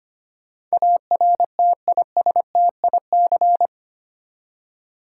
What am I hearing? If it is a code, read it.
Morse code, heard as a keyed tone, about 25 wpm, 700 Hz. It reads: ARTISTIC
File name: Eight Letter Words - 25wpm